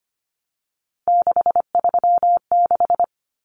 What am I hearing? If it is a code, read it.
Morse code, heard as a keyed tone, about 25 wpm, 700 Hz. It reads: 636
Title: Common Cut Numbers - 3-Digit 25wpm